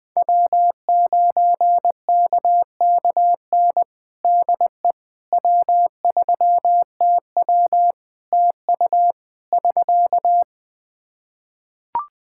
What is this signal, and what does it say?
Morse code, heard as a keyed tone, about 20 wpm, 700 Hz. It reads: W9KKN DE W3TW TU <SK>
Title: Practice QSO - Exchange 4 - Ending - 20wpm